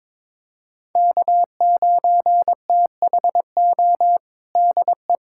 Morse code, heard as a keyed tone, about 22 wpm, 700 Hz. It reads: K9THO DE
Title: USA Callsign DE Callsign 22wpm